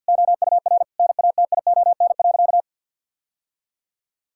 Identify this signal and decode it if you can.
Morse code, heard as a keyed tone, about 52 wpm, 700 Hz. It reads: OUR NATION.
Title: Sentences from Top 1000 Words - Review Part 6 - 50wpm